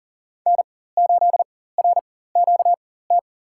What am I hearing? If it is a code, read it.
Morse code, heard as a keyed tone, about 40 wpm, spaced as for 19 wpm, 700 Hz. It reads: N8RQT
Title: USA Call Signs - 5 Characters - 4x Character Spacing - 40wpm